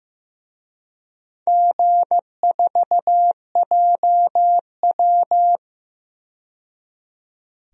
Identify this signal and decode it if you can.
Morse code, heard as a keyed tone, about 15 wpm, 700 Hz. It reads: G4JW
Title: DXCC-100 Call Signs - 4 Characters - 15wpm